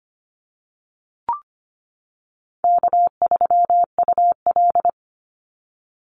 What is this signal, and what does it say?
Morse code, heard as a keyed tone, about 25 wpm, 700 Hz. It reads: K3UL
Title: Easy Call Signs - 25wpm